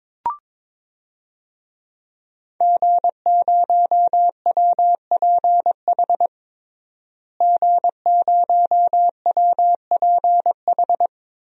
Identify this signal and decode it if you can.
Morse code, heard as a keyed tone, about 22 wpm, 700 Hz. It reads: G0WPH G0WPH
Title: DXCC-100 Call Signs - 5 Characters - Repeated 3x - 22wpm